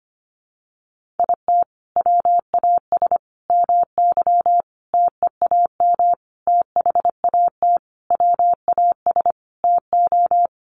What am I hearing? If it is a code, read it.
Morse code, heard as a keyed tone, about 25 wpm, 700 Hz. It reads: IT WAS MY TEAM THAT WAS TO
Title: Sentences from Top 1500 Words 25wpm